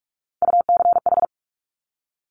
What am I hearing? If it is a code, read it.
Morse code, heard as a keyed tone, about 46 wpm, 700 Hz. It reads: UCH
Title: Top Three Letter Combinations in Words 45wpm